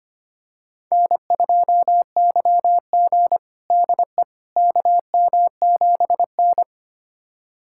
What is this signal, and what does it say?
Morse code, heard as a keyed tone, about 25 wpm, 700 Hz. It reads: N2YG DE KM7N
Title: USA Callsign DE Callsign - 4 Characters - 25wpm